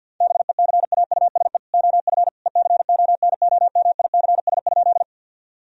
Morse code, heard as a keyed tone, about 50 wpm, 700 Hz. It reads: BECAUSE OF ECONOMICS?
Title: Sentences from Top 1000 Words - Review Part 3 - 50wpm